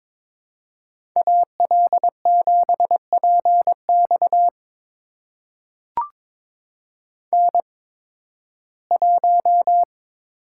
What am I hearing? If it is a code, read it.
Morse code, heard as a keyed tone, about 22 wpm, spaced as for 14 wpm, 700 Hz. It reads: AL7PX N1
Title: USA Call Signs - 5 Characters - 8x Character Spacing - 22wpm